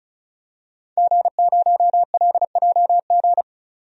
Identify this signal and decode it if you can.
Morse code, heard as a keyed tone, about 36 wpm, 700 Hz. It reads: G0LJG
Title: DXCC-100 Call Signs - 5 Characters - 35wpm